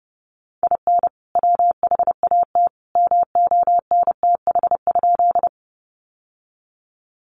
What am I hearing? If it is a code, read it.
Morse code, heard as a keyed tone, about 30 wpm, 700 Hz. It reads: IN WHAT MONTH?
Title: Sentences from Top 800 Words 30wpm